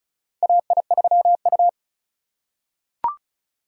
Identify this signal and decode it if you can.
Morse code, heard as a keyed tone, about 34 wpm, 700 Hz. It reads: AI3U
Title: USA Call Signs - 4 Characters - 2x Character Spacing - 35wpm